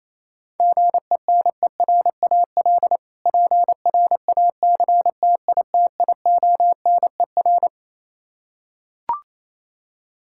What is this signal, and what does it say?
Morse code, heard as a keyed tone, about 28 wpm, 700 Hz. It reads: GENERAL PRACTITIONER
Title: Practice QSO - Exchange 3 - More Info - 28wpm